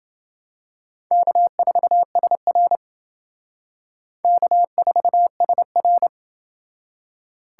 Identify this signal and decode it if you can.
Morse code, heard as a keyed tone, about 29 wpm, 700 Hz. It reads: K4SR K4SR